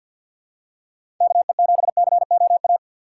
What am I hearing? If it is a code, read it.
Morse code, heard as a keyed tone, about 52 wpm, 700 Hz. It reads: KE7COA
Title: USA Call Signs - 4x Character Spacing - 50wpm